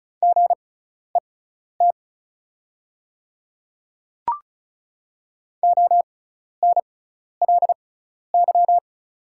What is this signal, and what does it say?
Morse code, heard as a keyed tone, about 36 wpm, spaced as for 12 wpm, 690 Hz. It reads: GET ONLY
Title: 6x Character Spacing - Top 300 Words - 35wpm